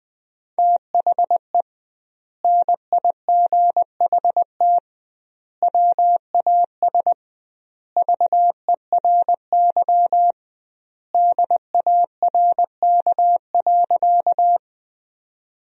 Morse code, heard as a keyed tone, about 20 wpm, 700 Hz. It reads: THE NIGHT WAS VERY DARK.
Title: 2x Word-Spacing - Sentences from Top 500 Words - 20wpm